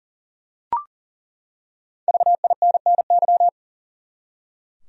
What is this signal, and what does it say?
Morse code, heard as a keyed tone, about 40 wpm, 700 Hz. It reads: VINNY
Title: USA Common Names - 40wpm